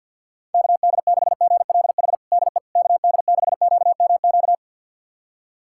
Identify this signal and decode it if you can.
Morse code, heard as a keyed tone, about 51 wpm, 690 Hz. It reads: KD6GLH DE KD6QMQ
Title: USA Callsign DE Callsign - 6 Characters - 50wpm